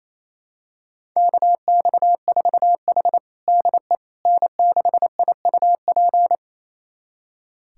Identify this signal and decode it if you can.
Morse code, heard as a keyed tone, about 27 wpm, 700 Hz. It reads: KX4H DE N6IUP
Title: USA Callsign DE Callsign 28wpm